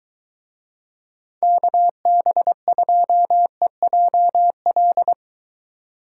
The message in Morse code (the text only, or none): KB2EJL